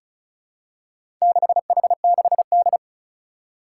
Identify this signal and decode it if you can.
Morse code, heard as a keyed tone, about 36 wpm, 700 Hz. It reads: 6H6D
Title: Four Challenging Letter-Number Combinations 35wpm